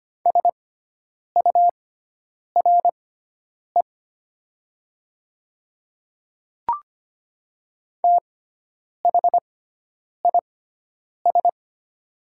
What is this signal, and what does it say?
Morse code, heard as a keyed tone, about 25 wpm, spaced as for 9 wpm, 700 Hz. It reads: SURE THIS
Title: 6x Character Spacing - Top 300 Words - 25wpm